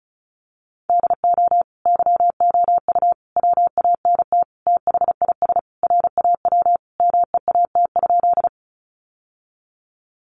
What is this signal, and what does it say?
Morse code, heard as a keyed tone, about 34 wpm, 700 Hz. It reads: DO YOU WANT THIS RAW MEAT?
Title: Sentences from Top 1500 Words 35wpm